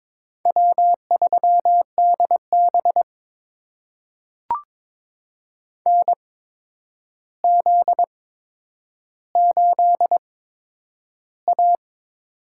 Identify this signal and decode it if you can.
Morse code, heard as a keyed tone, about 22 wpm, spaced as for 9 wpm, 700 Hz. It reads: W3DB NZ8A
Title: USA Call Signs - 8x Character Spacing - 22wpm